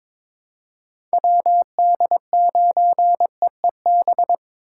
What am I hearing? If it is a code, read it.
Morse code, heard as a keyed tone, about 22 wpm, 700 Hz. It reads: WD9EEB
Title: USA Call Signs - 8x Character Spacing - 22wpm